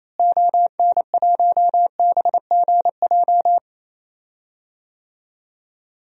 Morse code, heard as a keyed tone, about 28 wpm, 700 Hz. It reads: ON1BGJ
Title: DXCC-100 Call Signs - 6 Characters - Repeated 2x - 28wpm